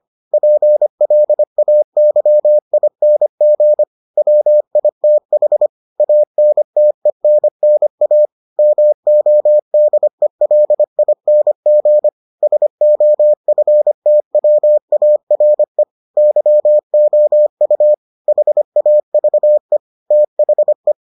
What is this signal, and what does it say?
Morse code, heard as a keyed tone, about 25 wpm, 600 Hz. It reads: PLAYING WITH ANTENNA MODELING SOFTWARE YOU HAVE THE